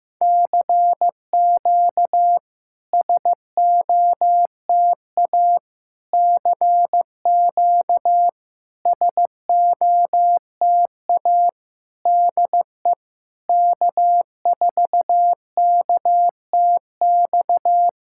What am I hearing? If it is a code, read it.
Morse code, heard as a keyed tone, about 15 wpm, 700 Hz. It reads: CQ SOTA CQ SOTA DE K4KTX
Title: SOTA QSO Practice 15wpm